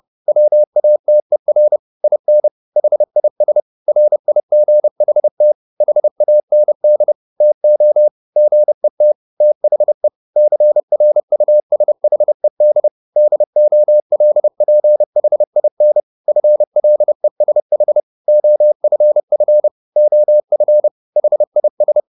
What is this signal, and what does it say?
Morse code, heard as a keyed tone, about 30 wpm, 600 Hz. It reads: WATER IN HIS RIGHT HAND TO GET THE CRUSHED DOLPHIN FLESH OFF OF HIS